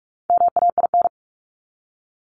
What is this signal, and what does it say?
Morse code, heard as a keyed tone, about 44 wpm, 700 Hz. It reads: MAIN
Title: Top 300 Words 45wpm